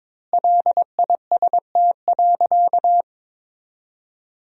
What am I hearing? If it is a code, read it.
Morse code, heard as a keyed tone, about 22 wpm, 700 Hz. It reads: LIST.
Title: 2x Word-Spacing - Sentences from Top 400 Words - 22wpm